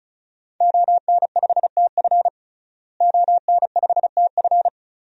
Morse code, heard as a keyed tone, about 36 wpm, 700 Hz. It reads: ON5TF ON5TF
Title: DXCC-100 Call Signs - 5 Characters - Repeated 2x - 35wpm